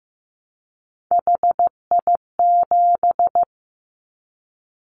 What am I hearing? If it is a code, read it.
Morse code, heard as a keyed tone, about 15 wpm, 700 Hz. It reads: HI7
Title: Callsign Prefixes for Top 100 DXCC - 15wpm